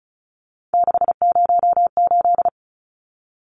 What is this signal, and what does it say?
Morse code, heard as a keyed tone, about 34 wpm, 700 Hz. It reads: 608
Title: Numbers - 3-Digit 35wpm